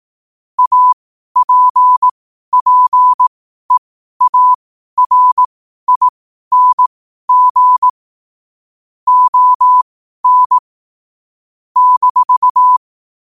Morse code, heard as a keyed tone, about 18 wpm, spaced as for 12 wpm, 1000 Hz. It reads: APPEARING ON -
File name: qotdfiles / 12wpm